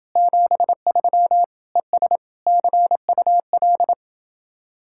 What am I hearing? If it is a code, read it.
Morse code, heard as a keyed tone, about 29 wpm, 700 Hz. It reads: 73 ES CUL